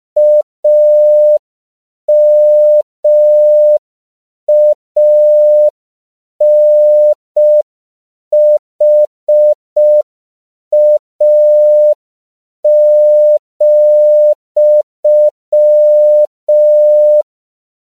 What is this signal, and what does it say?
Morse code, heard as a keyed tone, about 5 wpm, 600 Hz. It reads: AMANHA,
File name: 5PPM